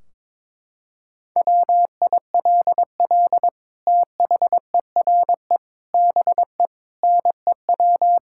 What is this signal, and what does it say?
Morse code, heard as a keyed tone, about 22 wpm, 700 Hz. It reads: WILL THERE BE NEW
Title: Sentences from Top 1600 Words 22wpm